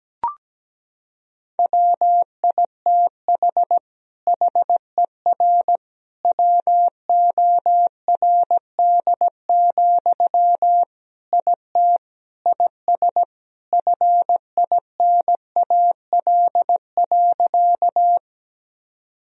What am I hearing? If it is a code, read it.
Morse code, heard as a keyed tone, about 17 wpm, 700 Hz. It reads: WITH HER WORD, IT IS FINAL.
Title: Sentences from Top 1000 Words - Review Part 2 - 17wpm